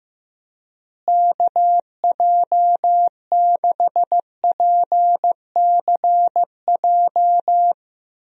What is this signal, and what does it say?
Morse code, heard as a keyed tone, about 15 wpm, 700 Hz. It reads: KJ6PCJ
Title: USA Call Signs - 6 Characters - Repeated 2x - 15wpm